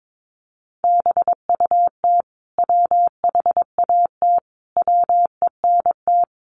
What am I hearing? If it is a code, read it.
Morse code, heard as a keyed tone, about 22 wpm, 700 Hz. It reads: BUT WHAT WENT
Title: Sentences from Top 1400 Words 22wpm